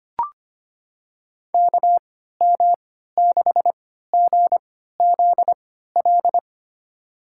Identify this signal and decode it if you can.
Morse code, heard as a keyed tone, about 25 wpm, spaced as for 14 wpm, 700 Hz. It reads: KM6GZL
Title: USA Call Signs - 6 Characters - 3x Character Spacing - 25wpm